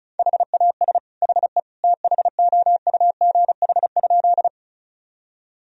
Morse code, heard as a keyed tone, about 34 wpm, 700 Hz. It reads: HAS HE THOUGH?
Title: Sentences from Top 1000 Words - Review Part 2 - 35wpm